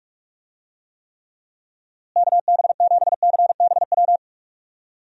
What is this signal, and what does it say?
Morse code, heard as a keyed tone, about 46 wpm, 700 Hz. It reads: KB7CBW
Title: USA Call Signs - 6 Characters - Repeated 3x - 45wpm